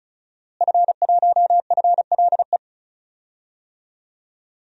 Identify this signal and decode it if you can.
Morse code, heard as a keyed tone, about 34 wpm, 700 Hz. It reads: F1FLE
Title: DXCC-100 Call Signs - 5 Characters - Repeated 2x - 35wpm